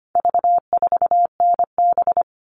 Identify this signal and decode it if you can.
Morse code, heard as a keyed tone, about 25 wpm, 700 Hz. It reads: V4NB
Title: All Cut Numbers - 4-Digit 25wpm